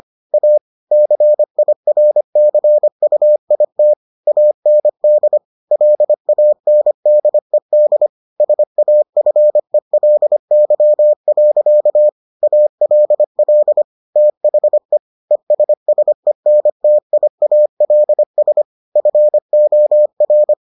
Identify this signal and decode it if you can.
Morse code, heard as a keyed tone, about 25 wpm, 600 Hz. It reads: A CIRCUIT AND LANDED SAFELY. ALL THE ESSENTIALS FOR